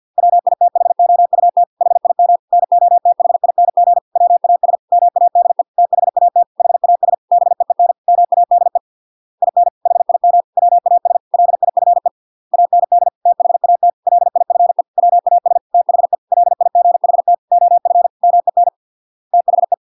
Morse code, heard as a keyed tone, about 50 wpm, 700 Hz. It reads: WITHOUT HIM NOTHING WAS MADE THAT HAS BEEN MADE IN HIM WAS LIFE AND THAT LIFE WAS THE LIGHT OF MEN THE